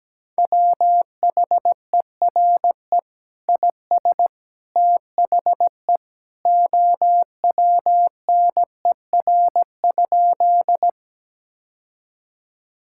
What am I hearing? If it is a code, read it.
Morse code, heard as a keyed tone, about 17 wpm, 700 Hz. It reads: WHERE IS THE OWNER?
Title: Sentences from Top 1600 Words 17wpm